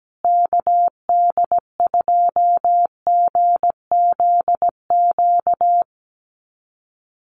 Morse code, heard as a keyed tone, about 17 wpm, 700 Hz. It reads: KD2GZQ
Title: USA Call Signs Repeated 3x 17wpm